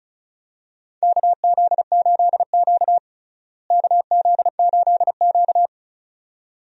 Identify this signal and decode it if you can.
Morse code, heard as a keyed tone, about 34 wpm, 700 Hz. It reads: KZ8Q KZ8Q
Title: USA Call Signs - 4 Characters - Repeated 2x - 35wpm